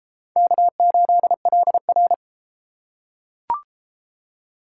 Morse code, heard as a keyed tone, about 32 wpm, 700 Hz. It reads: K8LR